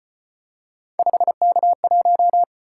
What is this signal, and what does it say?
Morse code, heard as a keyed tone, about 33 wpm, 700 Hz. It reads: 5K1